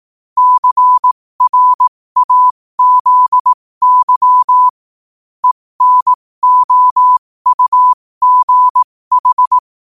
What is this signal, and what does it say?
Morse code, heard as a keyed tone, about 18 wpm, spaced as for 15 wpm, 1000 Hz. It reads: CRAZY ENOUGH
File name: qotdfiles / 15wpm